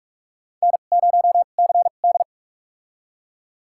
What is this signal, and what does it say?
Morse code, heard as a keyed tone, about 45 wpm, spaced as for 32 wpm, 700 Hz. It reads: N0CD
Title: USA Call Signs - 4 Characters - 2x Character Spacing - 45wpm